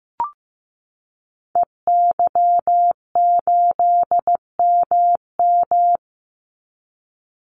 Morse code, heard as a keyed tone, about 15 wpm, 700 Hz. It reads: EY8MM
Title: Call Signs Repeated 2x - 15wpm